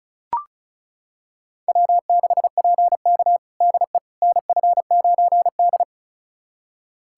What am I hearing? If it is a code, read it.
Morse code, heard as a keyed tone, about 34 wpm, 700 Hz. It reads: W6PK DE NF9D
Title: USA Callsign DE Callsign - 4 Characters - 35wpm